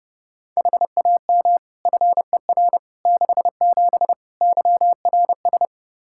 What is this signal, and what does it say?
Morse code, heard as a keyed tone, about 31 wpm, 700 Hz. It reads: HAM FER 67 YRS
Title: Practice QSO - Exchange 3 - More Info - 30wpm